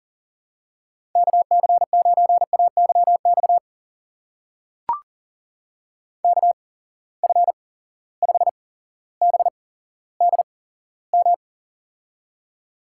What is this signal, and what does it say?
Morse code, heard as a keyed tone, about 40 wpm, spaced as for 17 wpm, 700 Hz. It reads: KC9AYX KF5BDM
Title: USA Call Signs - 6 Characters - 8x Character Spacing - 40wpm